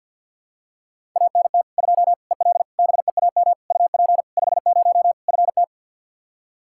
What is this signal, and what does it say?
Morse code, heard as a keyed tone, about 53 wpm, 700 Hz. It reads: ANT 2 EL BEAM UP 50 FT
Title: Practice QSO - Exchange 2 - Info - 50wpm